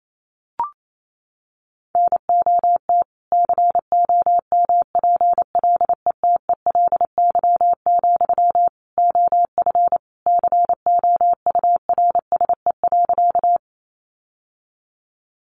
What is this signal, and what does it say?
Morse code, heard as a keyed tone, about 28 wpm, 700 Hz. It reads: NOT COMPLETELY, OF COURSE.
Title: Sentences from Top 1100 Words 28wpm